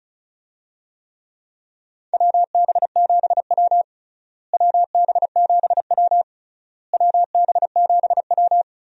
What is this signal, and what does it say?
Morse code, heard as a keyed tone, about 36 wpm, 700 Hz. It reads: WB7W WB7W WB7W
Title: USA Call Signs - 4 Characters - Repeated 3x - 35wpm